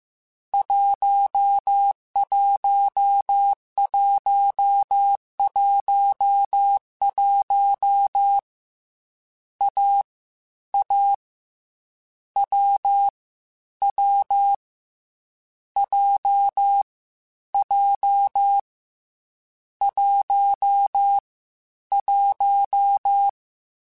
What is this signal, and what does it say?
Morse code, heard as a keyed tone, about 15 wpm, 800 Hz. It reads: 11111 A A W W J J 1 1
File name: Difficult Combos / Difficult Combos-mp3s / AWJ1 @13